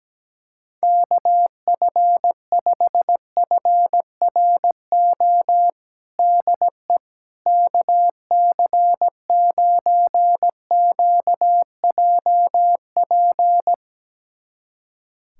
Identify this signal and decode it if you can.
Morse code, heard as a keyed tone, about 17 wpm, 700 Hz. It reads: KF5FRO DE KC9QJP
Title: USA Callsign DE Callsign - 6 Characters - 17wpm